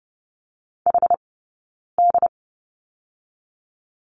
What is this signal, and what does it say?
Morse code, heard as a keyed tone, about 30 wpm, 700 Hz. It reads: H D
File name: course / 30:10wpm / 086 Lesson 15d - Number 5 - Two Characters 30wpm